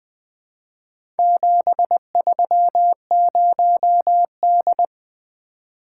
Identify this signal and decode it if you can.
Morse code, heard as a keyed tone, about 20 wpm, 700 Hz. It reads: 730D
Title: Four Letter-Number Combinations - 20wpm